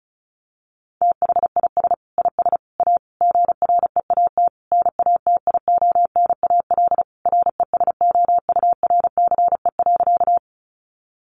Morse code, heard as a keyed tone, about 34 wpm, 700 Hz. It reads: THIS IS A GREAT NATIONAL RESOURCE.
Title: Sentences from Top 1300 Words 35wpm